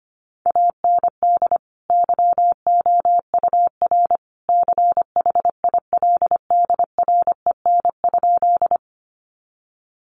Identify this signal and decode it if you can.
Morse code, heard as a keyed tone, about 25 wpm, 700 Hz. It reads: AND YOUR CHILDREN?
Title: Sentences from Top 300 Words 25wpm